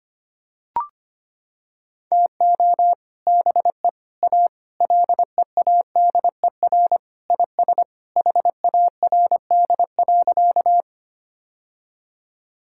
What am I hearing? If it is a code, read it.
Morse code, heard as a keyed tone, about 25 wpm, 700 Hz. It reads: TO BE A LEADER IS HARD.
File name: practice / 25wpm / Sentences with 3-Word Phrases - 25wpm